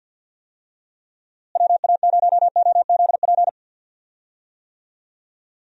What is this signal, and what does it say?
Morse code, heard as a keyed tone, about 50 wpm, 700 Hz. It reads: WA0OZP
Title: USA Call Signs - 6 Characters - Repeated 2x - 50wpm